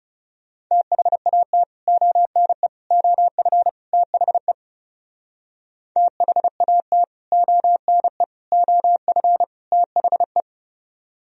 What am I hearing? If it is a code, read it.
Morse code, heard as a keyed tone, about 37 wpm, 700 Hz. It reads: THAT ONE OF THE THAT ONE OF THE
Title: Speed-Racing - Sets of 4 Words 30wpm